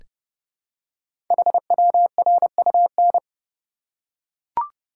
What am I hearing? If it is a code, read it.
Morse code, heard as a keyed tone, about 30 wpm, 700 Hz. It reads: HWRUN
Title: Code Groups - 5 Letters - 30wpm